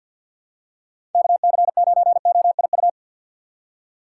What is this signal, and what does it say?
Morse code, heard as a keyed tone, about 47 wpm, 700 Hz. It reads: K C 9 O I U
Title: USA Call Signs - 6x Character Spacing - 50wpm